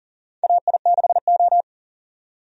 Morse code, heard as a keyed tone, about 40 wpm, 700 Hz. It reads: AI6O
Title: USA Call Signs - 4 Characters - 40wpm